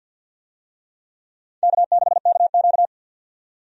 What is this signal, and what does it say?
Morse code, heard as a keyed tone, about 50 wpm, 700 Hz. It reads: K6KQ
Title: USA Call Signs - 4 Characters - 50wpm